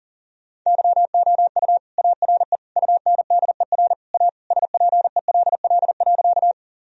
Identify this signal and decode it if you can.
Morse code, heard as a keyed tone, about 40 wpm, 700 Hz. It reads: YOU ARE UNDER A SPELL.
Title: Sentences from Top 1000 Words - Review Part 1 - 40wpm